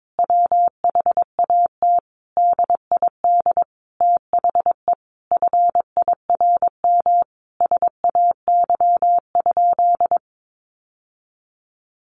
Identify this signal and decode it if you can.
Morse code, heard as a keyed tone, about 22 wpm, 700 Hz. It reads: WHAT DID THE FIRM SAY?
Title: Sentences from Top 1400 Words 22wpm